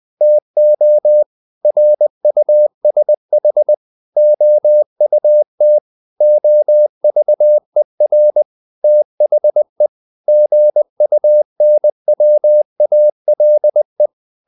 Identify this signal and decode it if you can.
Morse code, heard as a keyed tone, about 20 wpm, 600 Hz. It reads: TO RUSH OUT OVER THE GUNWALE
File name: the old man and the sea / the old man and the sea part 1 20wpm